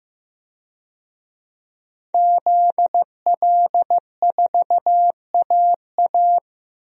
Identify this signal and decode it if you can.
Morse code, heard as a keyed tone, about 15 wpm, 700 Hz. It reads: ZL4AA
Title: DXCC-100 Call Signs - 5 Characters - Repeated 3x - 15wpm